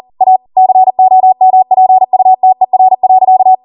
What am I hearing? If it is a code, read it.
Morse code, heard as a keyed tone, about 40 wpm, 750 Hz. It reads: A COMPUTER.